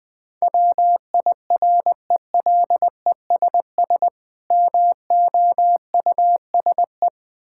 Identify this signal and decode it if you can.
Morse code, heard as a keyed tone, about 20 wpm, 700 Hz. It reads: WIRELESS MOUSE
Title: Difficult Words with Hints - 20wpm